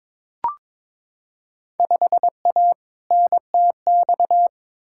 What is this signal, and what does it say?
Morse code, heard as a keyed tone, about 22 wpm, 700 Hz. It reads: 5A NTX
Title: Field Day Exchanges - 22wpm